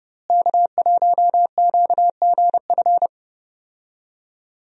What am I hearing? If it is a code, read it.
Morse code, heard as a keyed tone, about 32 wpm, 700 Hz. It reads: K1QGF
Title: USA Call Signs - 5 Characters - Repeated 3x - 30wpm